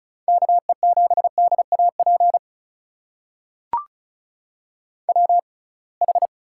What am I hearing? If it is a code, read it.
Morse code, heard as a keyed tone, about 36 wpm, spaced as for 27 wpm, 700 Hz. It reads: KE7DAP WH
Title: USA Call Signs - 6 Characters - 6x Character Spacing - 35wpm